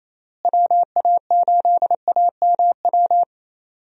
Morse code, heard as a keyed tone, about 27 wpm, 700 Hz. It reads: WA8AMW
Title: USA Call Signs - 6 Characters - Repeated 3x - 28wpm